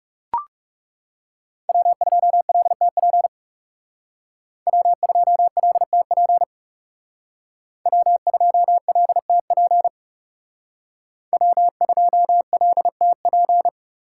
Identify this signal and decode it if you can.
Morse code, heard as a keyed tone, about 37 wpm, 700 Hz. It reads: W2LTP W2LTP W2LTP W2LTP
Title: Speed Racing - USA Call Signs 30wpm